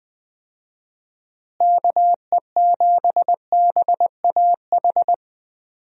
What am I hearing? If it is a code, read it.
Morse code, heard as a keyed tone, about 20 wpm, 700 Hz. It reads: KE7BAH
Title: USA Call Signs - 6 Characters - Repeated 2x - 20wpm